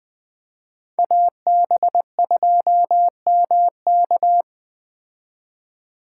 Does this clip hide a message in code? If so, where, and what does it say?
Yes: Morse code from 1.0 to 4.4 s, heard as a keyed tone, about 19 wpm, 700 Hz. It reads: AB2MK